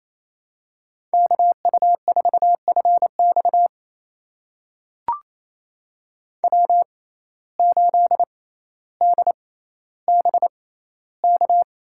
Morse code, heard as a keyed tone, about 28 wpm, spaced as for 14 wpm, 700 Hz. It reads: KU4FX W8DBK
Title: USA Call Signs - 5 Characters - 6x Character Spacing - 28wpm